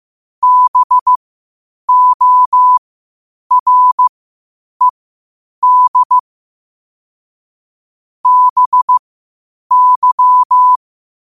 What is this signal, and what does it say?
Morse code, heard as a keyed tone, about 15 wpm, spaced as for 8 wpm, 1000 Hz. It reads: BORED BY
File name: qotdfiles / 8wpm